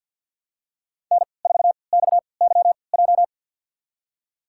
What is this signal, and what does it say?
Morse code, heard as a keyed tone, about 51 wpm, spaced as for 28 wpm, 700 Hz. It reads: N4XYJ
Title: USA Call Signs - 5 Characters - 3x Character Spacing - 50wpm